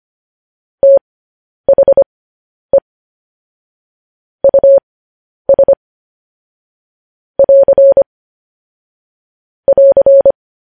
Morse code, heard as a keyed tone, about 25 wpm, spaced as for 10 wpm, 550 Hz. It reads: THE US <AR> <AR>